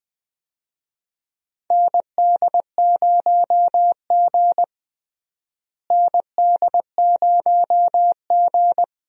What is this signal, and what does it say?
Morse code, heard as a keyed tone, about 20 wpm, 700 Hz. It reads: ND0G ND0G
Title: USA Call Signs - 4 Characters - Repeated 2x - 20wpm